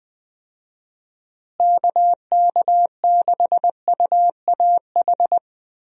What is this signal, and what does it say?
Morse code, heard as a keyed tone, about 20 wpm, 700 Hz. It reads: KK6UAH
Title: USA Call Signs - 6 Characters - Repeated 3x - 20wpm